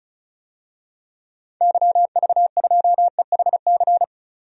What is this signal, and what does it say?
Morse code, heard as a keyed tone, about 36 wpm, 700 Hz. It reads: YV2EHC
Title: DXCC-100 Call Signs - 6 Characters - 35wpm